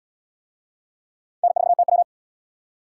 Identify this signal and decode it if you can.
Morse code, heard as a keyed tone, about 72 wpm, 700 Hz. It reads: N4TW